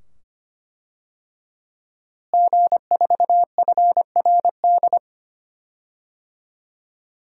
Morse code, heard as a keyed tone, about 25 wpm, 700 Hz. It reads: G4FRD
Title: DXCC-100 Call Signs - 5 Characters - Repeated 2x - 25wpm